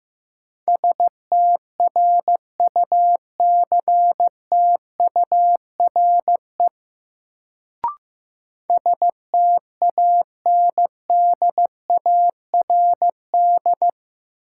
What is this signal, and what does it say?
Morse code, heard as a keyed tone, about 16 wpm, 700 Hz. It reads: STRUCTURE STANDARD
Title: Sentences from Top 1000 Words - Review Part 6 - New Word Drill - 15wpm